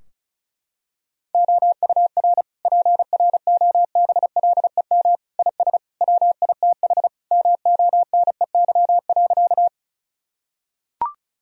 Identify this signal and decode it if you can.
Morse code, heard as a keyed tone, about 36 wpm, 700 Hz. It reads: OUR PROBLEM IS WITH MONEY.
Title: Sentences from Top 1000 Words - Review Part 2 - 35wpm